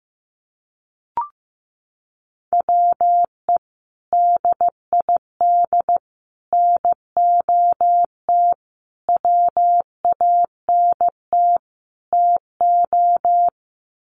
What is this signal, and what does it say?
Morse code, heard as a keyed tone, about 15 wpm, 700 Hz. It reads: WE DID NOT WANT TO
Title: Sets of 5 Words Encore 15wpm